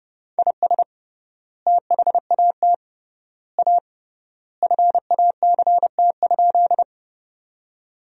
Morse code, heard as a keyed tone, about 28 wpm, 700 Hz. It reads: IS THAT A FACT?
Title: 3x Word-Spacing - Sentences from Top 500 Words - 30wpm